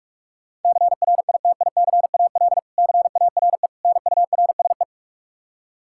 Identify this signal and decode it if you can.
Morse code, heard as a keyed tone, about 45 wpm, 700 Hz. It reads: CRITICAL CARE NURSE
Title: Practice QSO - Exchange 3 - More Info - 45wpm